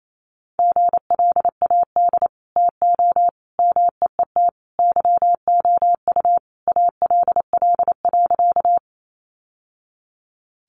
Morse code, heard as a keyed tone, about 28 wpm, 700 Hz. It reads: GLAD TO MEET YOU ALL.
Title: Sentences from Top 1100 Words 28wpm